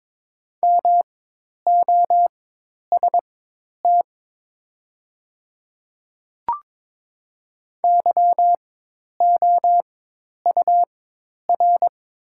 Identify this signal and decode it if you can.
Morse code, heard as a keyed tone, about 22 wpm, spaced as for 10 wpm, 700 Hz. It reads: MOST YOUR
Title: 4x Character Spacing - Top 100 Words - 22wpm